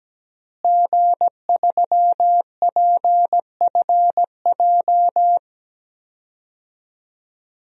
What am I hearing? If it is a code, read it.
Morse code, heard as a keyed tone, about 17 wpm, 700 Hz. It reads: G3PFJ
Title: DXCC-100 Call Signs - 5 Characters - Repeated 2x - 17wpm